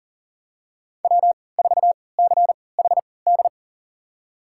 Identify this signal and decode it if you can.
Morse code, heard as a keyed tone, about 40 wpm, spaced as for 23 wpm, 700 Hz. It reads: W4CHD
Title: USA Call Signs - 5 Characters - 3x Character Spacing - 40wpm